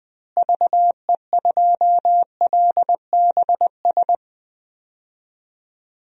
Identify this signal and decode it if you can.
Morse code, heard as a keyed tone, about 20 wpm, 700 Hz. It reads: VE2LBS
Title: DXCC-100 Call Signs - 6 Characters - Repeated 2x - 20wpm